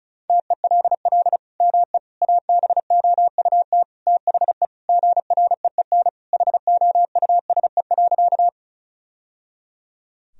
Morse code, heard as a keyed tone, about 34 wpm, 700 Hz. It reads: TELL ME ABOUT THE GREEN HOUSE.
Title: Sentences from Top 1000 Words - Review Part 2 - 35wpm